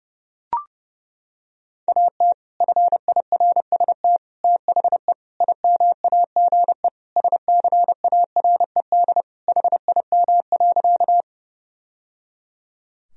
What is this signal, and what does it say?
Morse code, heard as a keyed tone, about 29 wpm, 700 Hz. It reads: AT FIRST THE IMAGE SCARED HIM.
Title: Sentences from Top 1000 Words - Review Part 4 - 30wpm